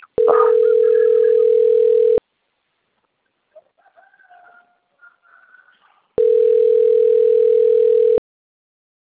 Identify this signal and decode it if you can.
Ringback tone (North America), 440 + 480 Hz.